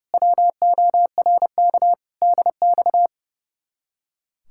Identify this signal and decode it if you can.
Morse code, heard as a keyed tone, about 36 wpm, 700 Hz. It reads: WORK DX
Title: Speed-Racing Common QSO Phrases - 30wpm